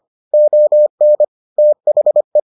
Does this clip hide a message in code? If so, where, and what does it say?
Yes: Morse code from 0.3 to 2.4 s, heard as a keyed tone, about 25 wpm, 600 Hz. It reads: ON THE